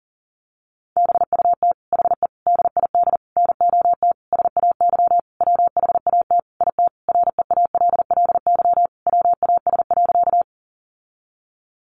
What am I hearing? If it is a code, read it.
Morse code, heard as a keyed tone, about 40 wpm, 700 Hz. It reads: BUT HE DID NOT SAY WHAT IT REALLY WAS.
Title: Sentences from Top 600 Words 40wpm